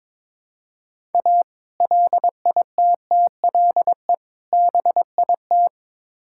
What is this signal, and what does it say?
Morse code, heard as a keyed tone, about 22 wpm, 700 Hz. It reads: A LITTLE BIT
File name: practice / 22wpm / Build A Phrase - 5-Word Phrases - 22wpm